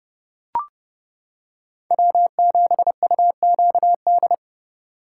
Morse code, heard as a keyed tone, about 29 wpm, 700 Hz. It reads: W7UQD